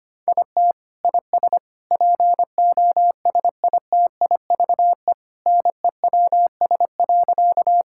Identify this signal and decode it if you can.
Morse code, heard as a keyed tone, about 25 wpm, 700 Hz. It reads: IT IS POSITIVE NEWS.
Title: Sentences from Top 1000 Words - Review Part 5 - 25wpm